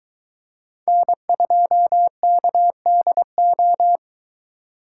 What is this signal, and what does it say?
Morse code, heard as a keyed tone, about 23 wpm, 700 Hz. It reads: N2KDO